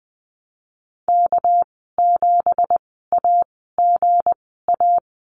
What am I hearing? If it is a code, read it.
Morse code, heard as a keyed tone, about 20 wpm, spaced as for 15 wpm, 700 Hz. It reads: K7AGA
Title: 207 Lesson 33g - Letter j - Call Signs 20wpm